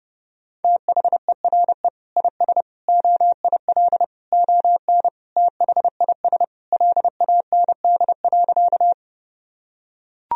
Morse code, heard as a keyed tone, about 28 wpm, 700 Hz. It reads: THERE IS OIL ON THIS LAND.
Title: Sentences from Top 1000 Words - Review Part 4 - 30wpm